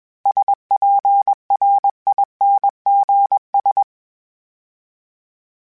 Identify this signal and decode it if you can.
Morse code, heard as a keyed tone, about 21 wpm, 800 Hz. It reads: SPRINGS
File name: fromSMs / FromSMs mp3s / Cities@20